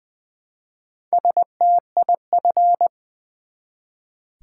Morse code, heard as a keyed tone, about 20 wpm, 700 Hz. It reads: STIF
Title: Build A Word - 1 Letter At A Time - 3 to 5-Letter Words - 20wpm